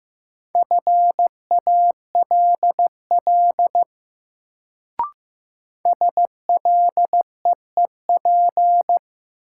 Morse code, heard as a keyed tone, about 15 wpm, 700 Hz. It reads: FALL SLEEP
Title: Sentences from Top 500 Words - New Word Drill - 15wpm